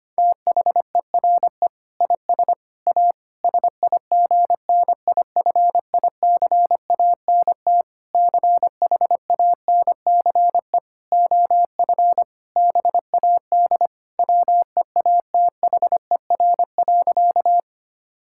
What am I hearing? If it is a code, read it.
Morse code, heard as a keyed tone, about 25 wpm, 700 Hz. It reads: THERE IS A SIGNIFICANT CHANCE OF BAD WEATHER.
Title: Sentences from Top 1000 Words - Review Part 6 - 25wpm